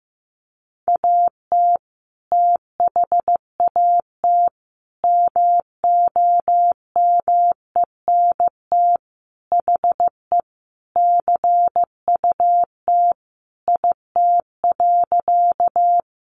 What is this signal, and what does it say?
Morse code, heard as a keyed tone, about 15 wpm, 700 Hz. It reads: AT THAT MOMENT HE CUT IT.
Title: Sentences from Top 1000 Words 15wpm